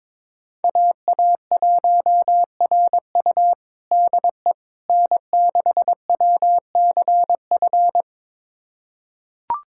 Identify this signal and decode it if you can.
Morse code, heard as a keyed tone, about 22 wpm, 700 Hz. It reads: AA1RU DE N6WCF
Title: USA Callsign DE Callsign - 5 Characters - 22wpm